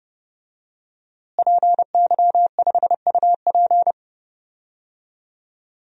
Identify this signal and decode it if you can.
Morse code, heard as a keyed tone, about 32 wpm, 700 Hz. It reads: PY5UP
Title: DXCC-100 Call Signs - 5 Characters - Repeated 3x - 30wpm